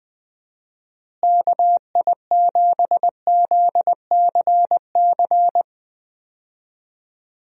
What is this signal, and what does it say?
Morse code, heard as a keyed tone, about 20 wpm, 700 Hz. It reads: KI7ZCC
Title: USA Call Signs - 6 Characters - Repeated 2x - 20wpm